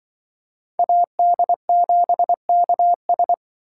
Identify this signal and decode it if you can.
Morse code, heard as a keyed tone, about 25 wpm, 700 Hz. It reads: AD7KS
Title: Speed Racing - USA 2x2 Call Signs 20wpm